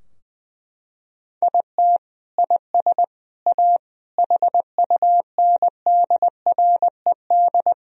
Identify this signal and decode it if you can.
Morse code, heard as a keyed tone, about 20 wpm, 700 Hz. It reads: IT IS A HUNDRED
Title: Sentences from Top 1000 Words - Review Part 2 - 20wpm